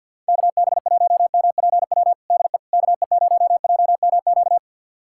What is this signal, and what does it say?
Morse code, heard as a keyed tone, about 51 wpm, 700 Hz. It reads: KB1MPW DE KE0JMQ
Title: USA Callsign DE Callsign - 6 Characters - 50wpm